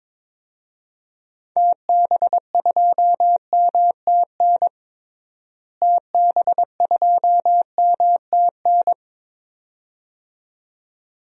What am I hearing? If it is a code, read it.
Morse code, heard as a keyed tone, about 22 wpm, 700 Hz. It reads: TB2MTN TB2MTN
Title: DXCC-100 Call Signs - 6 Characters - Repeated 2x - 22wpm